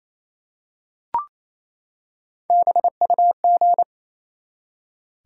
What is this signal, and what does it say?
Morse code, heard as a keyed tone, about 28 wpm, 700 Hz. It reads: BUG
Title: Three Letter Words - 28wpm